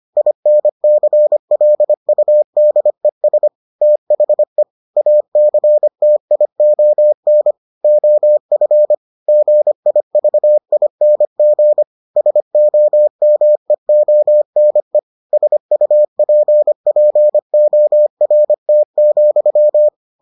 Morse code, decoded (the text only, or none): INCLUDES THE ACTION OF GIVING SOMEONE SUPPORT,